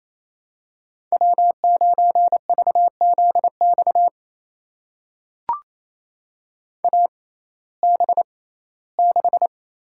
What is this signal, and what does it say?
Morse code, heard as a keyed tone, about 28 wpm, spaced as for 17 wpm, 700 Hz. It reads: W9VZX AB6
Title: USA Call Signs - 6x Character Spacing - 28wpm